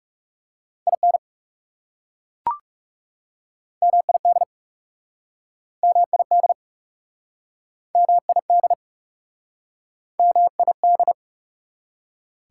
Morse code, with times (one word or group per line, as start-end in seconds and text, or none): 0.9-1.2 IN
3.8-4.4 MID
5.8-6.5 MID
8.0-8.7 MID
10.2-11.1 MID